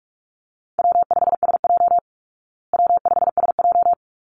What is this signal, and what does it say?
Morse code, heard as a keyed tone, about 44 wpm, 700 Hz. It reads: W5SJ W5SJ
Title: Call Signs Repeated 3x - 45wpm